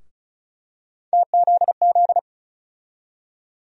Morse code, heard as a keyed tone, about 36 wpm, 700 Hz. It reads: TZZ
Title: USA Call Sign Postfixes - 3 Letters 35wpm